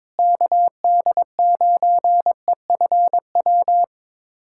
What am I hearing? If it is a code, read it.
Morse code, heard as a keyed tone, about 22 wpm, 700 Hz. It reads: KD9EFW
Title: USA Callsign DE Callsign - 6 Characters - 22wpm